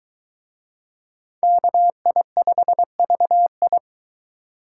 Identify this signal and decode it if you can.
Morse code, heard as a keyed tone, about 23 wpm, 700 Hz. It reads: KI5VI